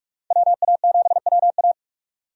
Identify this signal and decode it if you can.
Morse code, heard as a keyed tone, about 46 wpm, 700 Hz. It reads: WA7WA
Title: USA Call Signs - 5 Characters - 45wpm